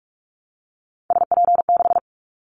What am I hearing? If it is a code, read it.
Morse code, heard as a keyed tone, about 46 wpm, 700 Hz. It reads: SP6
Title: Callsign Prefixes for Top 25 DXCC - 45wpm